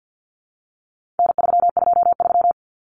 Morse code, heard as a keyed tone, about 50 wpm, 700 Hz. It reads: N323
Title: Common Cut Numbers - 4-Digit 50wpm